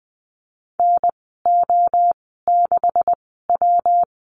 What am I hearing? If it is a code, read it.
Morse code, heard as a keyed tone, about 20 wpm, spaced as for 14 wpm, 700 Hz. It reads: NO6W
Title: 173 Lesson 28g - Number 9 - Call Signs 20wpm